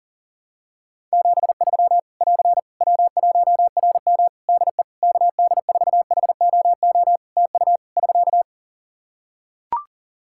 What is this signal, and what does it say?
Morse code, heard as a keyed tone, about 40 wpm, 700 Hz. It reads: 73 <AR> W1RM DE KD4HOO TU <SK>
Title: Practice QSO - Exchange 4 - Ending - 40wpm